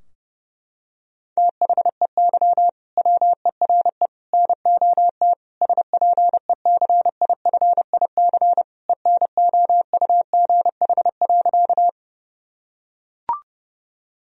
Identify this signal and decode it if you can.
Morse code, heard as a keyed tone, about 30 wpm, 700 Hz. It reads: THEY WERE NOT SPECIFIC ENOUGH.
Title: Sentences from Top 1000 Words - Review Part 3 - 30wpm